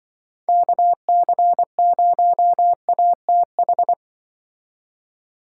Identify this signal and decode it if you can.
Morse code, heard as a keyed tone, about 24 wpm, 700 Hz. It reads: KC0ATH
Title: Speed Racing - USA 2x3 Call Signs 20wpm